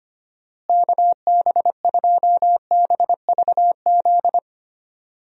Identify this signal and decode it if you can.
Morse code, heard as a keyed tone, about 25 wpm, 700 Hz. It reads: KB2BVZ
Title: USA Call Signs - 6 Characters - Repeated 2x - 25wpm